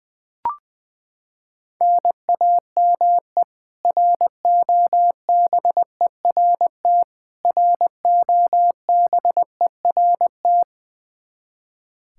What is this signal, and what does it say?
Morse code, heard as a keyed tone, about 21 wpm, 700 Hz. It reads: NAME ROBERT ROBERT
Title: Practice QSO - Exchange 1 - Beginning - 20wpm